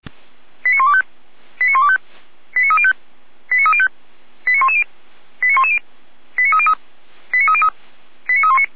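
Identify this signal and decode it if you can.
Five-tone selective call (ZVEI), heard as quick runs of five tones, tones of about 70 ms. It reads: 89125 89125 89305 89305 89110 89110 89303 89303 89319
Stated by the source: ZVEI five-tone selective call received by SDR, from the feuerwehr-orlamuende.org fire brigade server.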